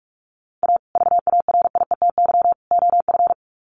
Morse code, heard as a keyed tone, about 44 wpm, 700 Hz. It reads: A VARIETY OF
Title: Sets of 3 Words 45wpm